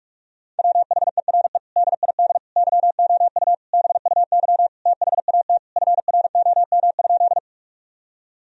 Morse code, heard as a keyed tone, about 46 wpm, 700 Hz. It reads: WHERE DID YOU BUY THAT FROM?
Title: Sentences from Top 1000 Words - Review Part 4 - 45wpm